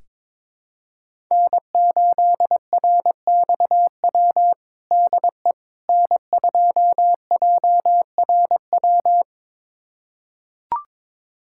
Morse code, heard as a keyed tone, about 22 wpm, 700 Hz. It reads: N8RXW DE N2JRW
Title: USA Callsign DE Callsign - 5 Characters - 22wpm